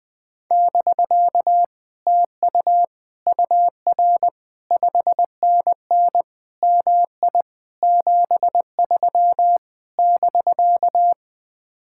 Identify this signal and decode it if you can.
Morse code, heard as a keyed tone, about 20 wpm, 700 Hz. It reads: <BK> TU UR 5NN MI 73 <BK>
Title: Speed-Racing Common QSO Phrases - 20wpm